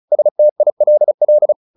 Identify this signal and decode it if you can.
Morse code, heard as a keyed tone, about 34 wpm, 600 Hz. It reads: STILL
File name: the old man and the sea / the old man and the sea 1 35wpm